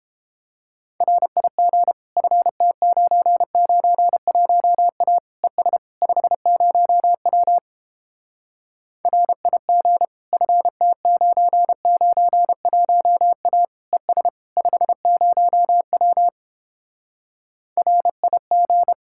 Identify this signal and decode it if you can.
Morse code, heard as a keyed tone, about 31 wpm, 700 Hz. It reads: RIG FT991A ES 50W RIG FT991A ES 50W RIG